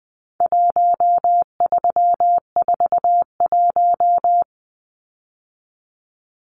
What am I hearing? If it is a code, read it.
Morse code, heard as a keyed tone, about 20 wpm, 700 Hz. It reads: 1341
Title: Numbers - 4-Digit 20wpm